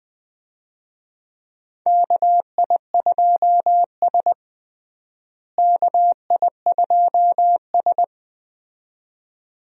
Morse code, heard as a keyed tone, about 20 wpm, 700 Hz. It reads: KI2S KI2S
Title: USA Call Signs - 4 Characters - Repeated 2x - 20wpm